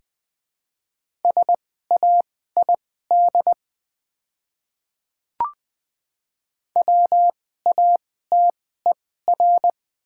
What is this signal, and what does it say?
Morse code, heard as a keyed tone, about 20 wpm, spaced as for 15 wpm, 700 Hz. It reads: SAID WATER
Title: 2x Character Spacing - Top 100 Words - 20wpm